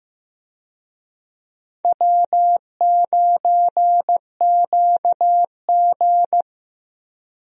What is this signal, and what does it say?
Morse code, heard as a keyed tone, about 15 wpm, 700 Hz. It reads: W9QG
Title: USA Call Signs - 4 Characters - Repeated 2x - 15wpm